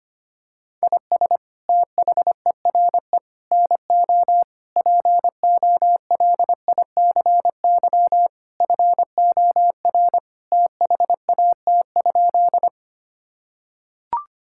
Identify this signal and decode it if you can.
Morse code, heard as a keyed tone, about 25 wpm, 700 Hz. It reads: IS THERE NO POLICY FOR THAT?
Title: Sentences from Top 1000 Words - Review Part 5 - 25wpm